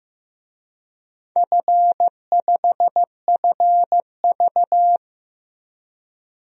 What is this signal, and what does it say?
Morse code, heard as a keyed tone, about 15 wpm, 700 Hz. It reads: F5FV
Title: DXCC-100 Call Signs - 4 Characters - 15wpm